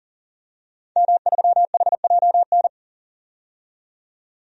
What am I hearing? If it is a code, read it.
Morse code, heard as a keyed tone, about 40 wpm, 700 Hz. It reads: M3HJN
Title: DXCC-100 Call Signs - 5 Characters - 40wpm